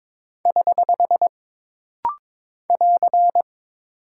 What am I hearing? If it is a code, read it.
Morse code, heard as a keyed tone, about 22 wpm, 700 Hz. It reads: <HH> <AR>
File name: practice / 22wpm / Single Punctuation-Prosign - Rapid-Fire - 22wpm